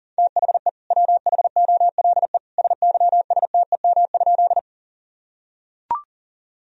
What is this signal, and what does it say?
Morse code, heard as a keyed tone, about 40 wpm, 700 Hz. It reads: THE WHOLE SYSTEM?
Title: Sentences from Top 1000 Words - Review Part 3 - 40wpm